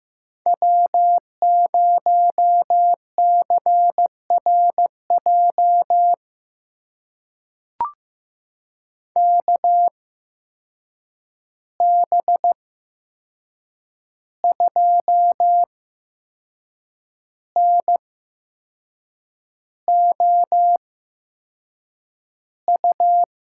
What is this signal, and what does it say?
Morse code, heard as a keyed tone, about 15 wpm, spaced as for 6 wpm, 700 Hz. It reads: W0CRJ KB2NOU
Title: USA Call Signs - 8x Character Spacing - 15wpm